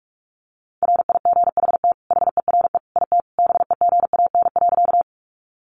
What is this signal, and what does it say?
Morse code, heard as a keyed tone, about 44 wpm, 700 Hz. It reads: RIGHT HERE IT BEGAN.
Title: Sentences from Top 300 Words 45wpm